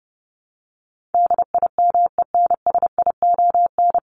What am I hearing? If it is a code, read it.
Morse code, heard as a keyed tone, about 30 wpm, 700 Hz. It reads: DIMENSION
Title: Top 1501-1750 Words 30wpm